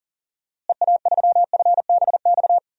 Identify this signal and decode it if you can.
Morse code, heard as a keyed tone, about 40 wpm, 700 Hz. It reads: EA3FBX
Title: DXCC-100 Call Signs - 6 Characters - Repeated 2x - 40wpm